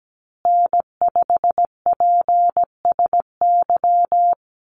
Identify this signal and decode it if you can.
Morse code, heard as a keyed tone, about 17 wpm, 700 Hz. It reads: N5PSY
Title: USA Call Signs Repeated 2x 17wpm